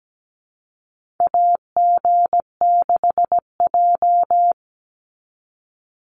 Morse code, heard as a keyed tone, about 17 wpm, 700 Hz. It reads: AG6J
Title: USA Call Signs 17wpm